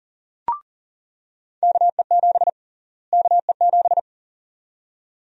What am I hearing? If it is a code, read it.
Morse code, heard as a keyed tone, about 37 wpm, 700 Hz. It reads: KE7 KE7
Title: USA Call Sign Prefixes - Repeated 2x - 40wpm